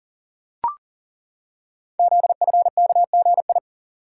Morse code, heard as a keyed tone, about 43 wpm, 690 Hz. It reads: ZFKGI
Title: Code Groups - 5 Letters - 40wpm